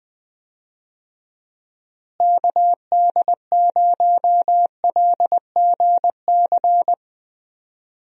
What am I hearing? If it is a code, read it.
Morse code, heard as a keyed tone, about 21 wpm, 700 Hz. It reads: KD0LGC
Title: USA Call Signs - 6 Characters - Repeated 2x - 20wpm